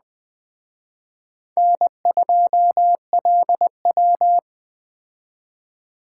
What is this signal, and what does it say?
Morse code, heard as a keyed tone, about 21 wpm, 700 Hz. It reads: N2LW
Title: Speed Racing - USA 1x2 Call Signs 17wpm